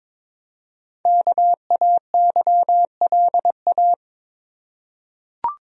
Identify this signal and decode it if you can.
Morse code, heard as a keyed tone, about 22 wpm, 700 Hz. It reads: KAYLA
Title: USA Common Names - 22wpm